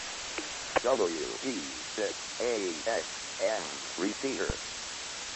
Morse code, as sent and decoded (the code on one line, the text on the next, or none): none